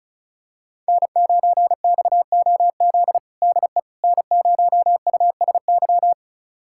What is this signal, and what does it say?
Morse code, heard as a keyed tone, about 34 wpm, 700 Hz. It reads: N9XOZ DE N0USY
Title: USA Callsign DE Callsign - 5 Characters - 35wpm